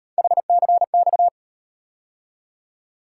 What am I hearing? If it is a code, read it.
Morse code, heard as a keyed tone, about 37 wpm, 700 Hz. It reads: HCX